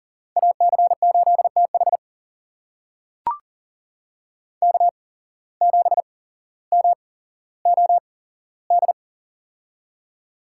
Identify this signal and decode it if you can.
Morse code, heard as a keyed tone, about 39 wpm, spaced as for 17 wpm, 700 Hz. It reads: AC8TH K7MOD
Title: USA Call Signs - 5 Characters - 8x Character Spacing - 40wpm